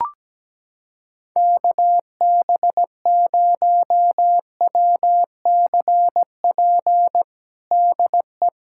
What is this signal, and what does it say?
Morse code, heard as a keyed tone, about 17 wpm, 700 Hz. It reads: KB0WCP DE
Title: USA Callsign DE Callsign - 6 Characters - 17wpm